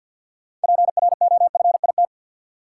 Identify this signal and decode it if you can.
Morse code, heard as a keyed tone, about 48 wpm, 700 Hz. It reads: PROFIT
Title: Sentences from Top 800 Words - New Word Drill - 50wpm